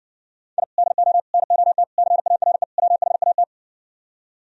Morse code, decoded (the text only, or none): I DO NOT CARE WHAT